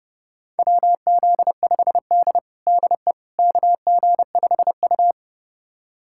Morse code, heard as a keyed tone, about 30 wpm, 700 Hz. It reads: WZ5D DE KG5U
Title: USA Callsign DE Callsign - 4 Characters - 30wpm